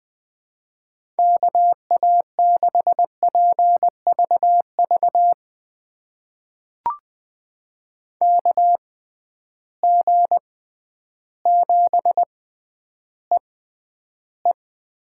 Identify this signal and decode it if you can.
Morse code, heard as a keyed tone, about 20 wpm, spaced as for 11 wpm, 700 Hz. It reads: KA6PVV KG7EE
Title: USA Call Signs - 6 Characters - 6x Character Spacing - 20wpm